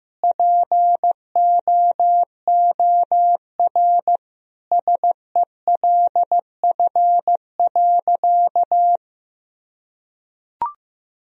Morse code, heard as a keyed tone, about 16 wpm, 700 Hz. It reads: POOR SELF.
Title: Sentences from Top 1000 Words - Review Part 1 - 15wpm